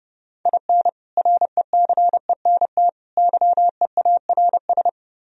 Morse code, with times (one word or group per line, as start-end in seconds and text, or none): 0.5-0.9 IN
1.2-2.9 RECENT
3.2-4.9 YEARS